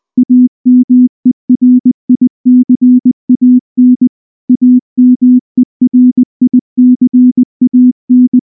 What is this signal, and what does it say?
Morse code, heard as a keyed tone, about 19 wpm, 250 Hz. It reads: AMERICAN AMERICAN
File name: morse / morse-training-20WPM-4